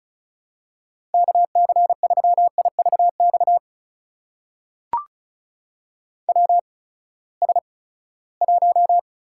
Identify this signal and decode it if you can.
Morse code, heard as a keyed tone, about 36 wpm, spaced as for 20 wpm, 700 Hz. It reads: KC3IVX WS1
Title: USA Call Signs - 8x Character Spacing - 35wpm